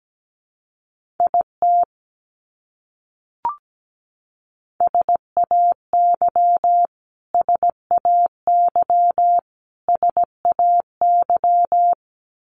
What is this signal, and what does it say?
Morse code, heard as a keyed tone, about 17 wpm, 700 Hz. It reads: IT SAY SAY SAY
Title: Top 300 Words Repeated 17wpm